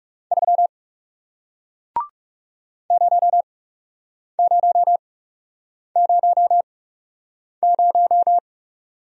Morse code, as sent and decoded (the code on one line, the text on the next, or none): ...-- / ----- ----- ----- -----
3 0000